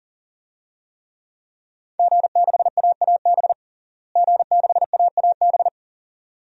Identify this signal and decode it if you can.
Morse code, heard as a keyed tone, about 40 wpm, 700 Hz. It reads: G6AAB G6AAB
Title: DXCC-100 Call Signs - 5 Characters - Repeated 2x - 40wpm